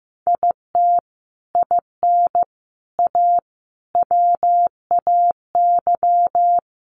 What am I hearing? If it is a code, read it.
Morse code, heard as a keyed tone, about 14 wpm, 700 Hz. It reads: IT IN A WAY
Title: Sets of 5 Words Encore2 15wpm